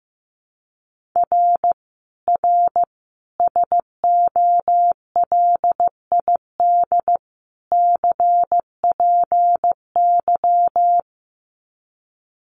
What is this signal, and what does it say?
Morse code, heard as a keyed tone, about 15 wpm, 700 Hz. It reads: R R SOLID CPY
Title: Common QSO Phrases 15wpm